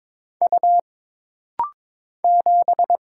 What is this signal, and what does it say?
Morse code, heard as a keyed tone, about 22 wpm, 700 Hz. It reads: U 7
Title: Single Letter-Number Alt - Rapid-Fire 22wpm